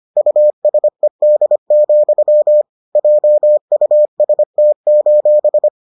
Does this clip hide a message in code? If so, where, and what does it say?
Yes: Morse code from 0.2 to 5.7 s, heard as a keyed tone, about 25 wpm, 600 Hz. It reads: USED, JUST: